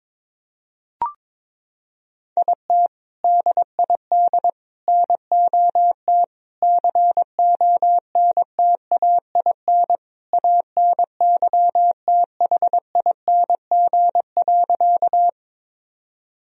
Morse code, heard as a keyed tone, about 22 wpm, 700 Hz. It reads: IT DID NOT CONTAIN ANYTHING.
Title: Sentences from Top 1000 Words - Review Part 2 - 22wpm